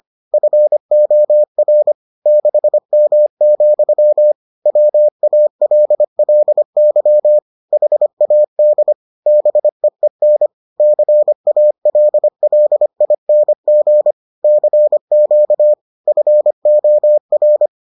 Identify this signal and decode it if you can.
Morse code, heard as a keyed tone, about 25 wpm, 600 Hz. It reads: FOR 6M, WALLY HAD BEEN CALLING CQ FOR